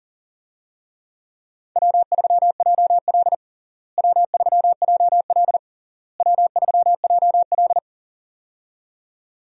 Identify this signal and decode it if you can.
Morse code, heard as a keyed tone, about 40 wpm, 700 Hz. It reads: W3JL W3JL W3JL
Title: USA Call Signs - 4 Characters - Repeated 3x - 40wpm